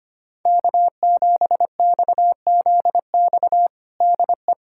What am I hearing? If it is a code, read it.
Morse code, heard as a keyed tone, about 25 wpm, 700 Hz. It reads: K7XZX DE
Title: USA Callsign DE Callsign 25wpm